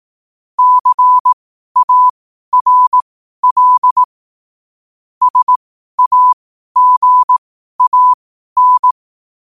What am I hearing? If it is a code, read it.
Morse code, heard as a keyed tone, about 18 wpm, spaced as for 12 wpm, 1000 Hz. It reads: CARL SAGAN
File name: qotdfiles / 12wpm